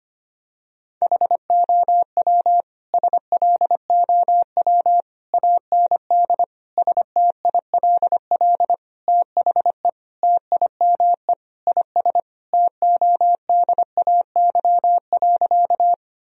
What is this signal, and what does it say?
Morse code, heard as a keyed tone, about 25 wpm, 700 Hz. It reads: HOW SLOW AND STILL THE TIME IS TODAY.
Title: Sentences from Top 1000 Words - Review Part 2 - 25wpm